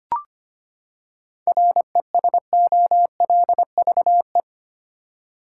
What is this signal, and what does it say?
Morse code, heard as a keyed tone, about 25 wpm, 700 Hz. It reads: RESOLVE
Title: Seven Letter Words - 25wpm